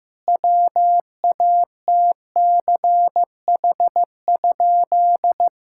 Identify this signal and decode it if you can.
Morse code, heard as a keyed tone, about 16 wpm, 700 Hz. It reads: WATCH?
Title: Sentences from Top 1600 Words 15wpm